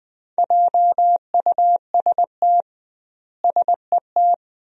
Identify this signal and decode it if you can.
Morse code, heard as a keyed tone, about 20 wpm, 700 Hz. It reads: JUST SET
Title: 2x Word-Spacing - Sentences from Top 300 Words - 20wpm